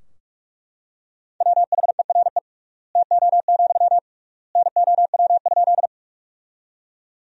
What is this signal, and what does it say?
Morse code, heard as a keyed tone, about 42 wpm, 700 Hz. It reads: WHERE TO, NOW?
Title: 3x Word-Spacing - Sentences from Top 200 Words - 45wpm